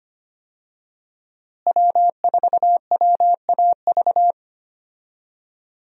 Morse code, heard as a keyed tone, about 25 wpm, 700 Hz. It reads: W4WAV
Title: USA Call Signs - 5 Characters - Repeated 2x - 25wpm